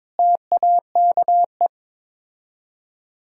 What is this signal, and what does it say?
Morse code, heard as a keyed tone, about 22 wpm, 700 Hz. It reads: TAKE
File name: practice / 22wpm / Build A Phrase - 5-Word Phrases - 22wpm